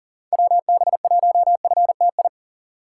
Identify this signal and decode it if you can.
Morse code, heard as a keyed tone, about 40 wpm, 700 Hz. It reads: WB1FTI
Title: USA Call Signs - 6 Characters - 4x Character Spacing - 40wpm